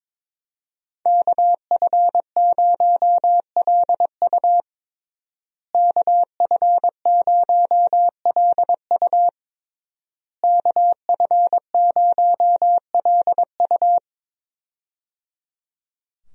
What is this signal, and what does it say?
Morse code, heard as a keyed tone, about 21 wpm, 700 Hz. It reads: KF0LU KF0LU KF0LU
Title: USA Call Signs - 5 Characters - Repeated 3x - 22wpm